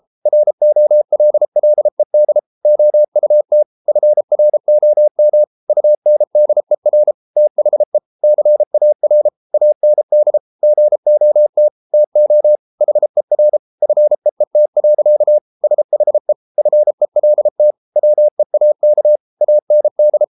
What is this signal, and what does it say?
Morse code, heard as a keyed tone, about 33 wpm, 600 Hz. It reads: ROLLED OUT FROM UNDER THE CAR AND GOT TO HER FEET. SHE FELT WEAK AND